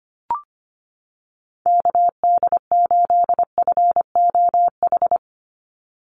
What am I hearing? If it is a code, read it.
Morse code, heard as a keyed tone, about 25 wpm, 700 Hz. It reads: KD8FOH
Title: USA Call Signs 25wpm